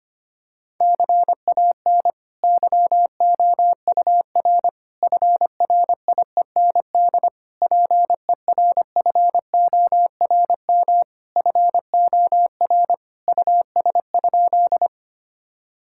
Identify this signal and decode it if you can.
Morse code, heard as a keyed tone, about 25 wpm, 700 Hz. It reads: CAN YOUR FRIEND PERFORM FOR US?
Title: Sentences from Top 1000 Words - Review Part 5 - 25wpm